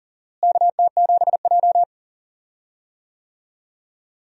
Morse code, heard as a keyed tone, about 40 wpm, 700 Hz. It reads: KT7J
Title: USA Call Signs - 4 Characters - Repeated 2x - 40wpm